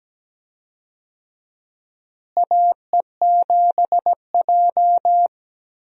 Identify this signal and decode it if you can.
Morse code, heard as a keyed tone, about 17 wpm, 700 Hz. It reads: AE7J
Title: USA Call Signs - 4 Characters - Repeated 2x - 17wpm